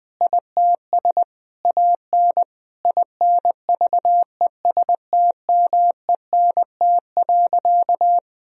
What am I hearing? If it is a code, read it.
Morse code, heard as a keyed tone, about 20 wpm, 700 Hz. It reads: ITS AN INVESTMENT.
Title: Sentences from Top 1000 Words - Review Part 4 - 20wpm